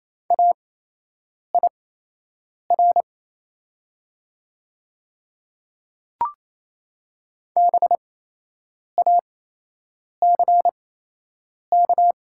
Morse code, heard as a keyed tone, about 28 wpm, spaced as for 8 wpm, 700 Hz. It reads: AIR BACK
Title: 8x Character Spacing - Top 500 Words - 28wpm